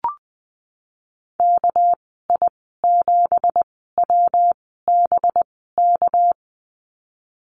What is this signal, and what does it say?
Morse code, heard as a keyed tone, about 20 wpm, spaced as for 15 wpm, 700 Hz. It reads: KI7WBK
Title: 207 Lesson 33g - Letter j - Call Signs 20wpm